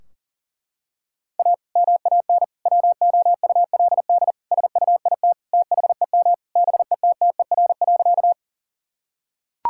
Morse code, heard as a keyed tone, about 40 wpm, 700 Hz. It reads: A MAN WOULD SUIT THEM BETTER.
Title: Sentences from Top 1600 Words 40wpm